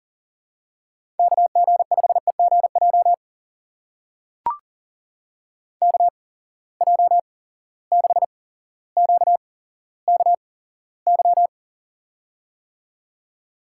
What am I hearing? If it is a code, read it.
Morse code, heard as a keyed tone, about 40 wpm, spaced as for 17 wpm, 700 Hz. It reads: KG5EGJ KJ6QKY
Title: USA Call Signs - 6 Characters - 8x Character Spacing - 40wpm